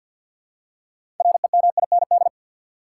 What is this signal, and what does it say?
Morse code, heard as a keyed tone, about 48 wpm, 700 Hz. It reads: REMIND
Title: Six Letter Words - 50wpm